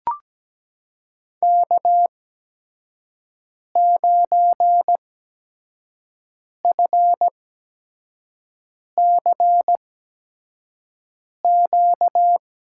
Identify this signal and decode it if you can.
Morse code, heard as a keyed tone, about 17 wpm, spaced as for 5 wpm, 700 Hz. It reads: K9FCQ
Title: USA Call Signs - 5 Characters - 8x Character Spacing - 17wpm